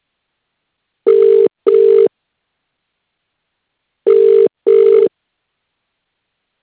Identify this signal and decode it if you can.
Ringback tone (UK), 400 + 450 Hz.